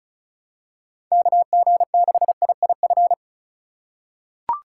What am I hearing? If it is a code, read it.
Morse code, heard as a keyed tone, about 34 wpm, 700 Hz. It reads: KG6IIF
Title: USA Call Signs - 6 Characters - 4x Character Spacing - 35wpm